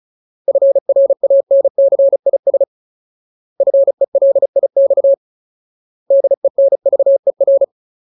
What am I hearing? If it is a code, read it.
Morse code, heard as a keyed tone, about 35 wpm, 550 Hz. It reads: FRANCIS FELIX DENVER
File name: Ultimate Word List / 35WPM Ultimate